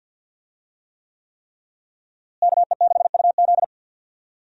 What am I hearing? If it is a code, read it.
Morse code, heard as a keyed tone, about 52 wpm, 700 Hz. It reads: KE6UZ
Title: USA Call Signs - 5 Characters - Repeated 3x - 50wpm